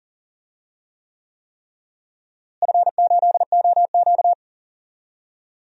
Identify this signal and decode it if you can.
Morse code, heard as a keyed tone, about 40 wpm, 700 Hz. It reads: F8OQ
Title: DXCC-100 Call Signs - 4 Characters - Repeated 3x - 40wpm